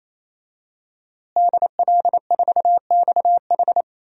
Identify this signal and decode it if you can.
Morse code, heard as a keyed tone, about 27 wpm, 700 Hz. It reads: DL4XH
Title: DXCC-100 Call Signs - 5 Characters - Repeated 3x - 28wpm